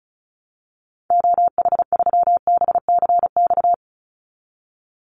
Morse code, heard as a keyed tone, about 36 wpm, 700 Hz. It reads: OH3BCX
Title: Call Signs - 35wpm